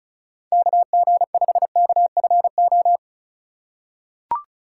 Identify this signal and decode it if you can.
Morse code, heard as a keyed tone, about 36 wpm, 700 Hz. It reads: KG5KFO
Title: USA Call Signs - 6 Characters - 2x Character Spacing - 35wpm